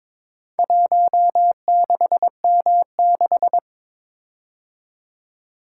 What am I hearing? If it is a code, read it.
Morse code, heard as a keyed tone, about 22 wpm, 700 Hz. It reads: 16M6
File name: practice / 22wpm / Four Letter-Number Combinations - 22wpm